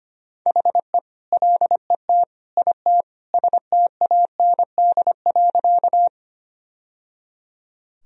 Morse code, heard as a keyed tone, about 25 wpm, 700 Hz. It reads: HE LET IT STAND.
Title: Sentences from Top 1000 Words - Review Part 1 - 25wpm